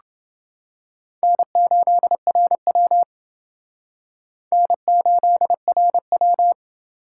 Morse code, heard as a keyed tone, about 29 wpm, 700 Hz. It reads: N8RW N8RW